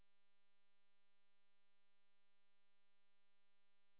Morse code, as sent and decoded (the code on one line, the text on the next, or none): none